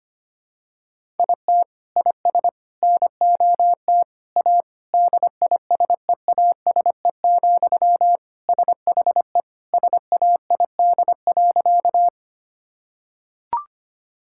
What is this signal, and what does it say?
Morse code, heard as a keyed tone, about 25 wpm, 700 Hz. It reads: IT IS NOT A DISEASE, SHE SAID.
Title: Sentences from Top 1000 Words - Review Part 6 - 25wpm